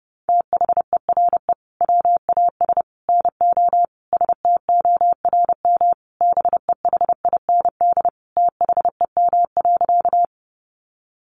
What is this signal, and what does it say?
Morse code, heard as a keyed tone, about 30 wpm, 700 Hz. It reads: THERE WAS NO STORM BEHIND THEM.
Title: Sentences from Top 1500 Words 30wpm